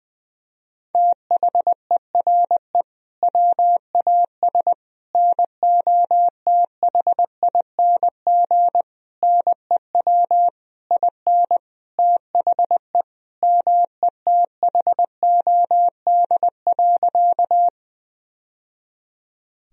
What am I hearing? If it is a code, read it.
Morse code, heard as a keyed tone, about 21 wpm, 700 Hz. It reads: THERE WAS NOTHING NEW IN THE METHOD.
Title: Sentences from Top 1000 Words - Review Part 3 - 20wpm